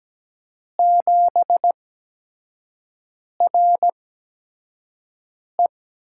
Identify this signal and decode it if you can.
Morse code, heard as a keyed tone, about 17 wpm, spaced as for 5 wpm, 700 Hz. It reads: 7RE
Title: USA Call Signs - 8x Character Spacing - 17wpm